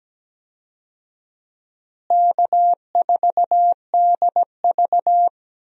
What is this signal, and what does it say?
Morse code, heard as a keyed tone, about 17 wpm, 700 Hz. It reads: K4DV
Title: USA Call Signs - 4 Characters - Repeated 2x - 17wpm